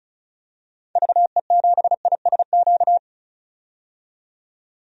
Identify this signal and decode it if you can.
Morse code, heard as a keyed tone, about 34 wpm, 700 Hz. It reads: VE7ISQ
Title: DXCC-100 Call Signs - 6 Characters - 35wpm